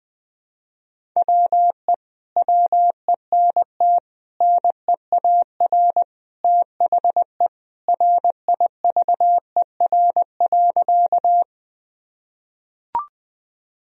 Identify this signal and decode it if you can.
Morse code, heard as a keyed tone, about 20 wpm, 700 Hz. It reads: WE WENT NEAR THE RIVER.
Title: Sentences from Top 1000 Words - Review Part 1 - 20wpm